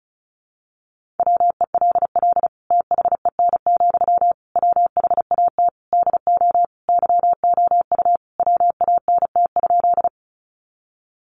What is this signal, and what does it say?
Morse code, heard as a keyed tone, about 34 wpm, 700 Hz. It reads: WELL THEN, WHAT DO YOU WANT?
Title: Sentences from Top 200 Words 35wpm